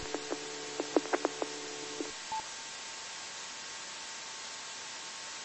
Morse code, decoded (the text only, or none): none